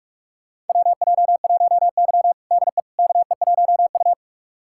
Morse code, heard as a keyed tone, about 46 wpm, 700 Hz. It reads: WJ1Y DE KE1U
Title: USA Callsign DE Callsign - 4 Characters - 45wpm